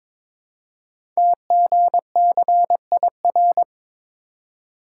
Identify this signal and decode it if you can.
Morse code, heard as a keyed tone, about 22 wpm, 700 Hz. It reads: TGCIR
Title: Code Groups - 5 Letters - 22wpm